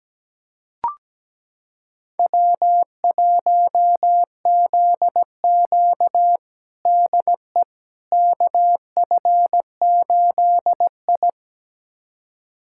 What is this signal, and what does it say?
Morse code, heard as a keyed tone, about 17 wpm, 700 Hz. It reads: W1ZQ DE KF8I
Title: USA Callsign DE Callsign - 4 Characters - 17wpm